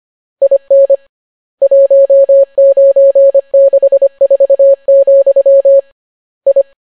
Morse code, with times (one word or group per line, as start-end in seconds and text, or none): 0.4-1.0 IN
1.6-5.8 1964,
6.5-6.6 I